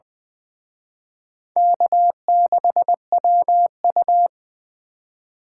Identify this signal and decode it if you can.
Morse code, heard as a keyed tone, about 25 wpm, 700 Hz. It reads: K6WU
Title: Speed Racing - USA 1x2 Call Signs 20wpm